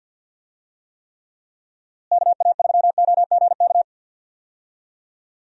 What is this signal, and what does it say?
Morse code, heard as a keyed tone, about 52 wpm, 700 Hz. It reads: KA3OGK
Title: USA Call Signs - 6 Characters - Repeated 3x - 50wpm